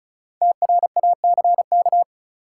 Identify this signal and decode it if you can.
Morse code, heard as a keyed tone, about 34 wpm, 700 Hz. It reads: TRACK
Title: Sentences from Top 1100 Words - New Word Drill Alt - 35wpm